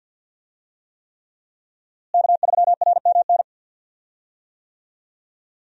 Morse code, heard as a keyed tone, about 51 wpm, 700 Hz. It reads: K3RMN
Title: USA Call Signs - 5 Characters - Repeated 3x - 50wpm